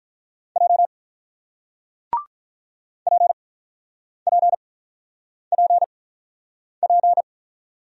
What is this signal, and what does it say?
Morse code, heard as a keyed tone, about 48 wpm, 700 Hz. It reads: J P P P P